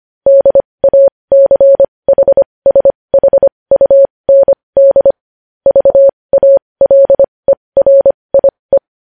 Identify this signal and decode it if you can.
Morse code, heard as a keyed tone, about 25 wpm, spaced as for 20 wpm, 550 Hz. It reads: DACHSHUND VALERIE
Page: news headlines in morse code at 20 wpm